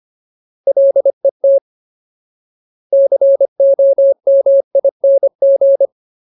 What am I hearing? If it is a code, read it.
Morse code, heard as a keyed tone, about 25 wpm, 550 Hz. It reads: LET COMING
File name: Ultimate Word List / 25WPM Ultimate